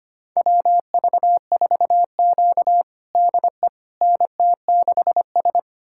Morse code, heard as a keyed tone, about 25 wpm, 700 Hz. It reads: WV4Q DE NT6S
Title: USA Callsign DE Callsign - 4 Characters - 25wpm